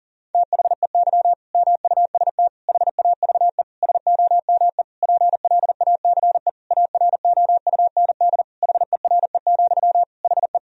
Morse code, heard as a keyed tone, about 40 wpm, 700 Hz. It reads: THEY MUST HAVE SOME PLACE AROUND HERE, HE
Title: Sentences from Top 1000 Words - Review Part 3 - 40wpm